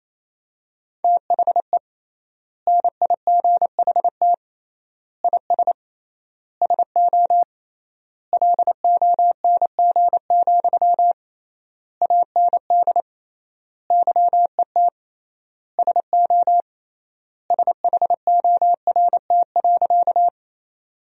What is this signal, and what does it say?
Morse code, heard as a keyed tone, about 28 wpm, 700 Hz. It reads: THE NIGHT IS SO LONG, AND YET SO SHORT.
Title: 3x Word-Spacing - Sentences from Top 500 Words - 28wpm